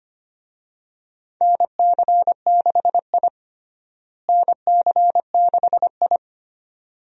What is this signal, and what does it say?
Morse code, heard as a keyed tone, about 25 wpm, 700 Hz. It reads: NC6I NC6I
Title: USA Call Signs - 4 Characters - Repeated 2x - 25wpm